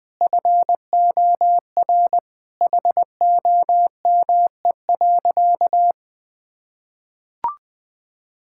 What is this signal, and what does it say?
Morse code, heard as a keyed tone, about 20 wpm, 700 Hz. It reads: FOR HOME.
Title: Sentences with 4-Word Phrases - 20wpm